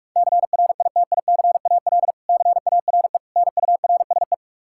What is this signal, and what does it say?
Morse code, heard as a keyed tone, about 46 wpm, 700 Hz. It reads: CRITICAL CARE NURSE
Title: Practice QSO - Exchange 3 - More Info - 45wpm